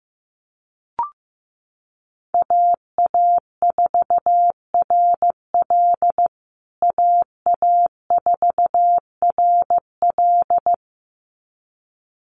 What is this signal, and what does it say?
Morse code, heard as a keyed tone, about 15 wpm, 700 Hz. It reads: AA4RL AA4RL
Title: USA Call Signs Repeated 2x 15wpm